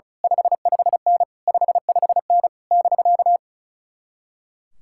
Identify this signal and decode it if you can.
Morse code, heard as a keyed tone, about 33 wpm, 700 Hz. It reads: 55N 55N <BK>
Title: Speed-Racing Common QSO Phrases - 35wpm